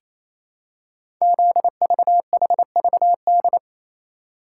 Morse code, heard as a keyed tone, about 27 wpm, 700 Hz. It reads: ZVHVD
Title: Code Groups - 5 Letters - 28wpm